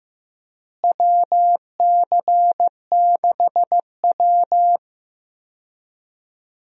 Morse code, heard as a keyed tone, about 15 wpm, 700 Hz. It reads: WC6W
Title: USA Call Signs - 4 Characters - Repeated 2x - 15wpm